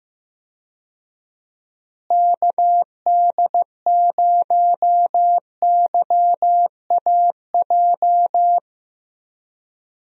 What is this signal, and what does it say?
Morse code, heard as a keyed tone, about 15 wpm, 700 Hz. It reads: KD0YAJ
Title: USA Call Signs - 6 Characters - Repeated 3x - 15wpm